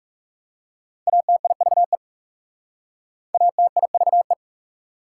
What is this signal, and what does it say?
Morse code, heard as a keyed tone, about 40 wpm, 690 Hz. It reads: ATIVE ATIVE